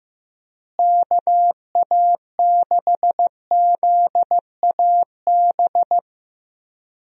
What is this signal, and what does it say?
Morse code, heard as a keyed tone, about 15 wpm, 700 Hz. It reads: KA6ZAB
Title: USA Call Signs - 6 Characters - Repeated 2x - 15wpm